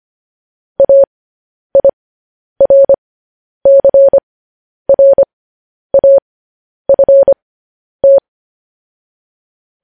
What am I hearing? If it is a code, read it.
Morse code, heard as a keyed tone, about 25 wpm, spaced as for 10 wpm, 550 Hz. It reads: AIRCRAFT
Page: News Headlines in Morse Code at 10 WPM